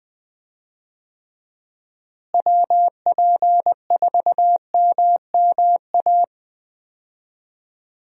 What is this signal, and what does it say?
Morse code, heard as a keyed tone, about 20 wpm, 700 Hz. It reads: WP4MMA
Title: USA Call Signs - 6 Characters - Repeated 2x - 20wpm